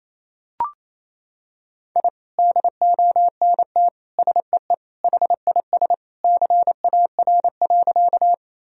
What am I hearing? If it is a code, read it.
Morse code, heard as a keyed tone, about 28 wpm, 700 Hz. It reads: I DONT SEE HIS CAR.
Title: Sentences from Top 1000 Words - Review Part 1 - 28wpm